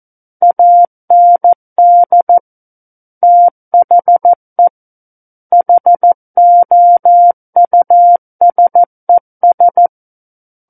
Morse code, heard as a keyed tone, about 14 wpm, 700 Hz. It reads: AND THE HOUSES
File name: Book One / 05-Chapter 5 - 14 WPM